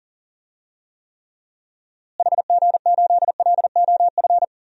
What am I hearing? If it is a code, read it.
Morse code, heard as a keyed tone, about 37 wpm, 700 Hz. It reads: HG8LOF
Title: DXCC-100 Call Signs - 6 Characters - Repeated 2x - 40wpm